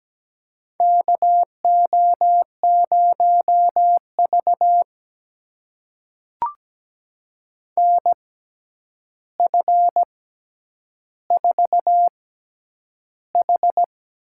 Morse code, heard as a keyed tone, about 17 wpm, spaced as for 9 wpm, 700 Hz. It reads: KO0V NF4H
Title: USA Call Signs - 4 Characters - 6x Character Spacing - 17wpm